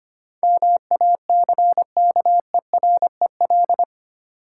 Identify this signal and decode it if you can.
Morse code, heard as a keyed tone, about 25 wpm, 700 Hz. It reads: MACKEREL
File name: practice / 25wpm / Difficult Words with Hints - 25wpm